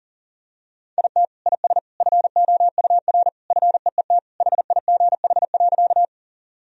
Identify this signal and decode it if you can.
Morse code, heard as a keyed tone, about 37 wpm, 700 Hz. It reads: IT IS FOUR FEET HIGH.
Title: Sentences from Top 1000 Words - Review Part 1 - 40wpm